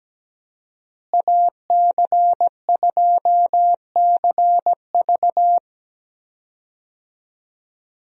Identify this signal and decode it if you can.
Morse code, heard as a keyed tone, about 17 wpm, 700 Hz. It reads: AC2CV
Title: USA Call Signs - 5 Characters - Repeated 3x - 17wpm